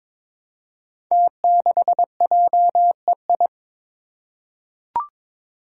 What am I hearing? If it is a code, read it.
Morse code, heard as a keyed tone, about 22 wpm, 700 Hz. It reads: T6JEI
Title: Five Challenging Letter-Number Combinations 22wpm